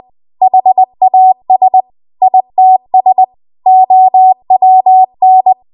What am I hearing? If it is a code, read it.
Morse code, heard as a keyed tone, about 20 wpm, 750 Hz. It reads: HAS ITS OWN
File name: Archive / 20 WPM / 20WPM